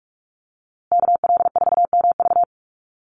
Morse code, heard as a keyed tone, about 44 wpm, 700 Hz. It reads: KL4MV
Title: USA Call Signs Repeated 2x 45wpm